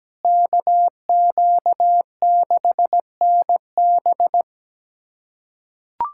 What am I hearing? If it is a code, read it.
Morse code, heard as a keyed tone, about 17 wpm, 700 Hz. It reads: KQ6NB
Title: USA Callsign DE Callsign - 5 Characters - 17wpm